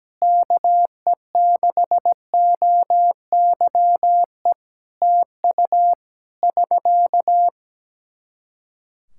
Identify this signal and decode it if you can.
Morse code, heard as a keyed tone, about 17 wpm, 700 Hz. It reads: KE6OYE TU <SK>
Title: Practice QSO - Exchange 4 - Ending - 17wpm